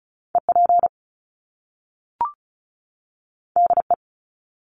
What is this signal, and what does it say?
Morse code, heard as a keyed tone, about 36 wpm, 700 Hz. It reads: EP DE
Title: Top Two Letter Combinations in Words 35wpm